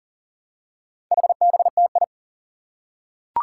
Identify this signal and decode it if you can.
Morse code, heard as a keyed tone, about 40 wpm, 700 Hz. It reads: HBTI
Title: Four Challenging Letter-Number Combinations 40wpm